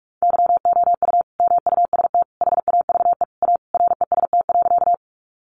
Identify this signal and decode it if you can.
Morse code, heard as a keyed tone, about 44 wpm, 700 Hz. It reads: YOU MUST HAVE A REST.
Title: Sentences from Top 500 Words 45wpm